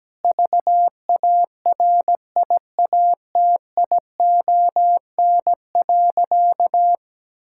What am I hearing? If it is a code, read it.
Morse code, heard as a keyed tone, about 17 wpm, 700 Hz. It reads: VARIATION.
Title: Sentences from Top 1600 Words 17wpm